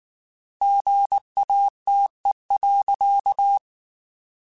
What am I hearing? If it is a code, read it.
Morse code, heard as a keyed tone, about 19 wpm, 800 Hz. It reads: GATE.
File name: Technical Subjects / Technical Subjects-mp3s / OR Gate @18